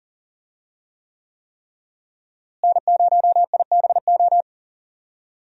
Morse code, heard as a keyed tone, about 40 wpm, 700 Hz. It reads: N0IBO
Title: USA Call Signs - 5 Characters - Repeated 3x - 40wpm